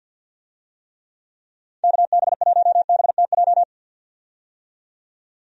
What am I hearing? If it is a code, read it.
Morse code, heard as a keyed tone, about 53 wpm, 700 Hz. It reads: KB1BTJ
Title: USA Call Signs - 6 Characters - 50wpm